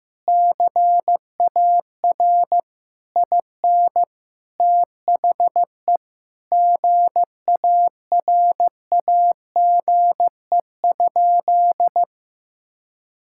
Morse code, heard as a keyed tone, about 15 wpm, 700 Hz. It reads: CAR IN THE GARAGE?
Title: Sentences from Top 1600 Words 15wpm